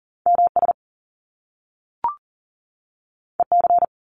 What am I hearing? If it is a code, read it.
Morse code, heard as a keyed tone, about 38 wpm, 700 Hz. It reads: MS EC
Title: Top Two Letter Combinations in Words 40wpm